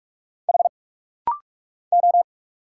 Morse code, heard as a keyed tone, about 46 wpm, 700 Hz. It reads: H O
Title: Single Letter-Number Alt - Rapid-Fire 45wpm